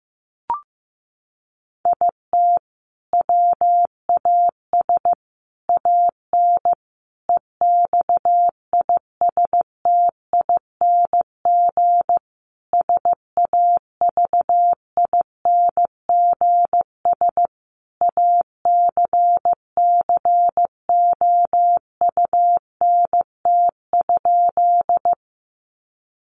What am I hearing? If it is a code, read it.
Morse code, heard as a keyed tone, about 15 wpm, 700 Hz. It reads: IT WAS AN EXISTING SAVINGS ACCOUNT?
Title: Sentences from Top 1500 Words 15wpm